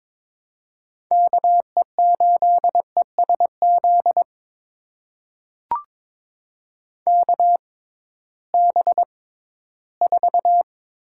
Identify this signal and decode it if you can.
Morse code, heard as a keyed tone, about 22 wpm, spaced as for 14 wpm, 700 Hz. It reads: KE8ESZ KB4
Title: USA Call Signs - 6 Characters - 6x Character Spacing - 22wpm